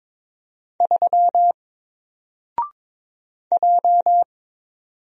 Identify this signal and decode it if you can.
Morse code, heard as a keyed tone, about 22 wpm, 700 Hz. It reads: 3 J
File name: Single-Letter-Number-22wpm